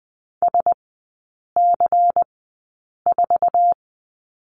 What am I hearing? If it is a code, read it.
Morse code, heard as a keyed tone, about 20 wpm, spaced as for 8 wpm, 700 Hz. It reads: SC4
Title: 138 Lesson 23e - Letter f - Three Characters 20wpm